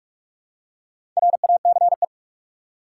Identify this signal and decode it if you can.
Morse code, heard as a keyed tone, about 46 wpm, 700 Hz. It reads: RACE
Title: Four Letter Words - 45wpm